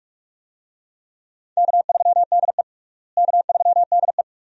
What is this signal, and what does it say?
Morse code, heard as a keyed tone, about 41 wpm, 700 Hz. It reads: K3DE K3DE
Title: USA Call Signs - 4 Characters - Repeated 2x - 45wpm